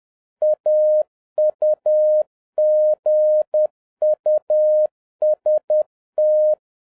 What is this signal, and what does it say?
Morse code, heard as a keyed tone, about 10 wpm, 610 Hz. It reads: AUGUST